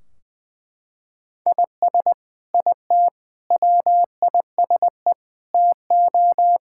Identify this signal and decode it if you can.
Morse code, heard as a keyed tone, about 20 wpm, 700 Hz. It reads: IS IT WISE TO
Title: Sentences from Top 1600 Words 20wpm